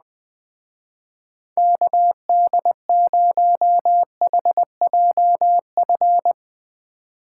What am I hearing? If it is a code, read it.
Morse code, heard as a keyed tone, about 20 wpm, 700 Hz. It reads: KD0HJF
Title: Speed Racing - USA 2x3 Call Signs 17wpm